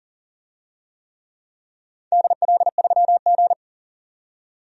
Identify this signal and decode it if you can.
Morse code, heard as a keyed tone, about 39 wpm, 700 Hz. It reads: DL3G
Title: DXCC-100 Call Signs - 4 Characters - 40wpm